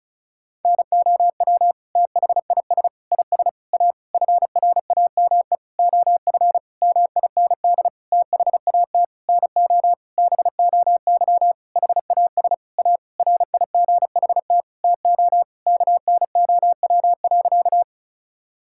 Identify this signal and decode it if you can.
Morse code, heard as a keyed tone, about 36 wpm, 690 Hz. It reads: NOW THIS IS A FRAME OF MIND THAT NO BOY HAS A RIGHT TO KNOW.
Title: Sentences from Top 1000 Words - Review Part 6 - 35wpm